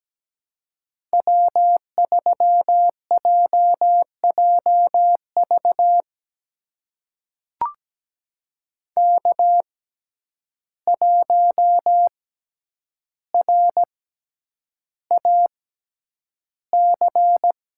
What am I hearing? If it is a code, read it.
Morse code, heard as a keyed tone, about 17 wpm, spaced as for 9 wpm, 700 Hz. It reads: W3JJV K1RAC
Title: USA Call Signs - 5 Characters - 6x Character Spacing - 17wpm